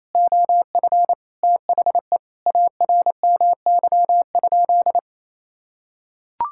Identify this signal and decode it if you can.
Morse code, heard as a keyed tone, about 28 wpm, 700 Hz. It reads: OF THE ARMY?
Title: Sentences from Top 1000 Words - Review Part 6 - 28wpm